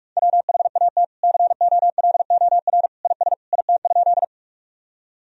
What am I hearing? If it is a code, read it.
Morse code, heard as a keyed tone, about 46 wpm, 700 Hz. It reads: WHAT COLOR IS IT?
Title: Sentences from Top 1000 Words - Review Part 2 - 45wpm